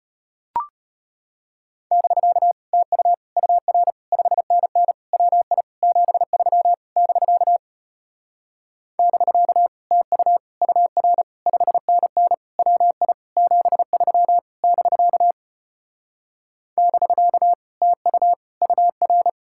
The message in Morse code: <BK> TU UR 5NN WI 73 <BK> <BK> TU UR 5NN WI 73 <BK> <BK> TU UR

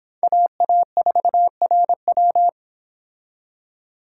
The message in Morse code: AA4RW

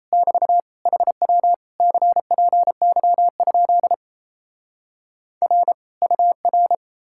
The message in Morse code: <BT> HW CPY? R UR

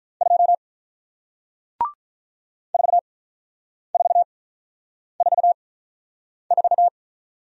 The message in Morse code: 2 4444

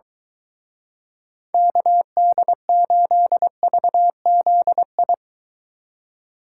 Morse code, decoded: KD8VZI

